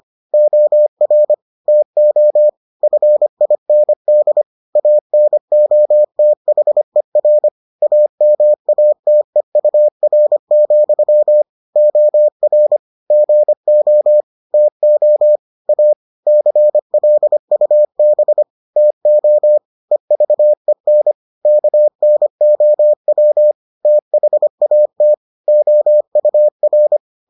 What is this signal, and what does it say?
Morse code, heard as a keyed tone, about 25 wpm, 600 Hz. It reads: OR TO FIND ANOTHER AMATEUR, OR GO TO A CLUB TO EVEN KNOW THAT OUR